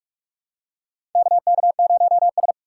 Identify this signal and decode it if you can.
Morse code, heard as a keyed tone, about 48 wpm, 700 Hz. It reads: KK0S
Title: USA Call Signs - 4 Characters - 2x Character Spacing - 45wpm